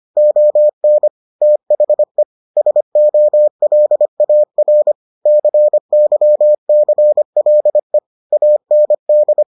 Morse code, heard as a keyed tone, about 25 wpm, 600 Hz. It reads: ON THE SOLAR CYCLE AND